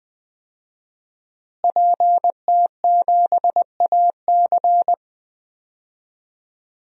Morse code, heard as a keyed tone, about 21 wpm, 700 Hz. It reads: PT7AC
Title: DXCC-100 Call Signs - 5 Characters - Repeated 3x - 20wpm